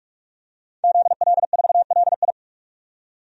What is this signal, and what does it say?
Morse code, heard as a keyed tone, about 48 wpm, 700 Hz. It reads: ZL4LI
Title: DXCC-100 Call Signs - 5 Characters - 45wpm